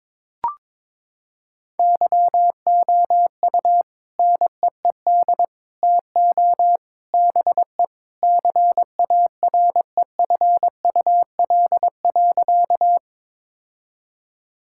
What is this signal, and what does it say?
Morse code, heard as a keyed tone, about 22 wpm, 700 Hz. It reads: YOU NEED TO BE CAREFUL.
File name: practice / 22wpm / Sentences with 3-Word Phrases - 22wpm